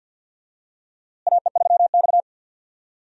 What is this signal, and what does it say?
Morse code, heard as a keyed tone, about 50 wpm, 700 Hz. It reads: AE3X
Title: USA Call Signs - 4 Characters - Repeated 2x - 50wpm